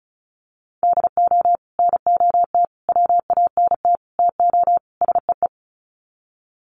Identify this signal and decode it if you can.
Morse code, heard as a keyed tone, about 34 wpm, 700 Hz. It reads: DO NOT WANT TO SEE
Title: Sets of 5 Words 35wpm